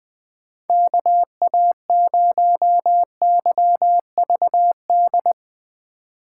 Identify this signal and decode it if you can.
Morse code, heard as a keyed tone, about 20 wpm, 700 Hz. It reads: KA0YVD
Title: USA Call Signs - 3x Character Spacing - 20wpm